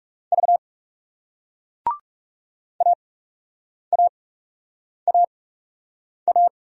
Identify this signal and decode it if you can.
Morse code, heard as a keyed tone, about 41 wpm, spaced as for 8 wpm, 700 Hz. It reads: V AAAA